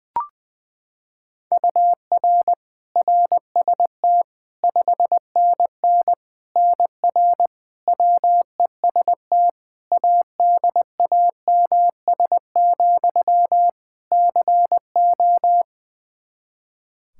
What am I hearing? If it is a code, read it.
Morse code, heard as a keyed tone, about 20 wpm, 700 Hz. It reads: UR RST 5NN NR WEST ADAMS, CO
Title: Practice QSO - Exchange 1 - Beginning - 20wpm